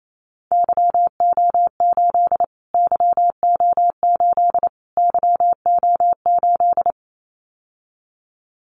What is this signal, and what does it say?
Morse code, heard as a keyed tone, about 28 wpm, 700 Hz. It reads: YO8 YO8 YO8
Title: Callsign Prefixes for Top 25 DXCC - Repeated 3x - 28wpm